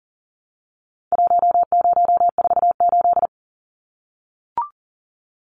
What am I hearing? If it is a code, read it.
Morse code, heard as a keyed tone, about 40 wpm, 700 Hz. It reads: 1048
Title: Numbers - Contest Style - 4-Digit 40wpm